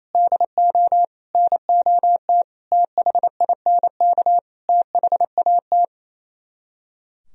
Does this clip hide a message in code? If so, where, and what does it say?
Yes: Morse code from 0.1 to 5.8 s, heard as a keyed tone, about 38 wpm, 700 Hz. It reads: DO NOT THINK THAT